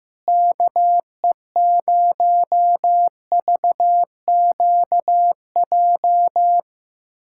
Morse code, heard as a keyed tone, about 15 wpm, 700 Hz. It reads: KE0VQJ
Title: USA Callsign DE Callsign 15wpm